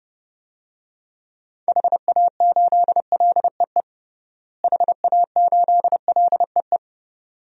Morse code, heard as a keyed tone, about 30 wpm, 700 Hz. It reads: HA8LEE HA8LEE
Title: DXCC-100 Call Signs - 6 Characters - Repeated 2x - 30wpm